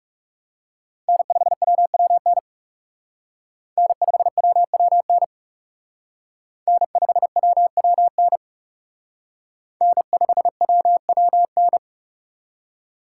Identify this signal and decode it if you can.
Morse code, heard as a keyed tone, about 36 wpm, 700 Hz. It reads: N5WWN N5WWN N5WWN N5WWN
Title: Speed Racing - USA Call Signs 30wpm